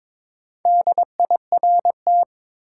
Morse code, heard as a keyed tone, about 22 wpm, 700 Hz. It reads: DIRT
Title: Four Letter Words - 22wpm